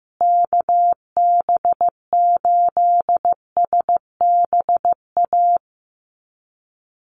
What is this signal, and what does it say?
Morse code, heard as a keyed tone, about 15 wpm, 700 Hz. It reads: KB8SBA
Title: USA Call Signs Repeated 3x 15wpm